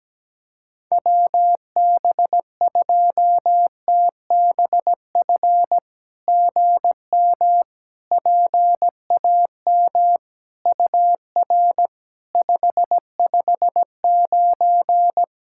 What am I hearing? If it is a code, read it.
Morse code, heard as a keyed tone, about 17 wpm, 700 Hz. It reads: WB2TBF GM PAM UR 559
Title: SOTA QSO Practice 17wpm